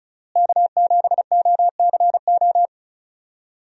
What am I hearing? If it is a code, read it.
Morse code, heard as a keyed tone, about 36 wpm, 700 Hz. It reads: K7OCO
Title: USA Call Signs - 5 Characters - Repeated 3x - 35wpm